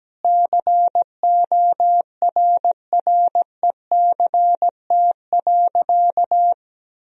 Morse code, heard as a keyed tone, about 17 wpm, 700 Hz. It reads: CORRECT.
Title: Sentences from Top 1000 Words - Review Part 2 - 17wpm